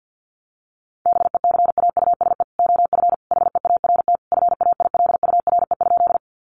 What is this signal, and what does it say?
Morse code, heard as a keyed tone, about 50 wpm, 700 Hz. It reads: BECAUSE OF HEART FAILURE?
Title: Sentences from Top 1200 Words 50wpm